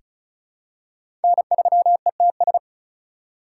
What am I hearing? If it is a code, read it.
Morse code, heard as a keyed tone, about 36 wpm, 700 Hz. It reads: N3ETS
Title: USA Call Signs - 5 Characters - 2x Character Spacing - 35wpm